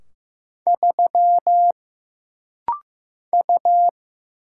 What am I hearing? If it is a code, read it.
Morse code, heard as a keyed tone, about 15 wpm, 700 Hz. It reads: 3 U
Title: Single Letter-Number Alt - Rapid-Fire 15wpm